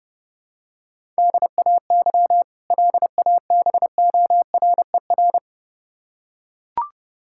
Morse code, heard as a keyed tone, about 30 wpm, 700 Hz. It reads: DAY LABORER
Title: K1USN SST QSO Practice 30wpm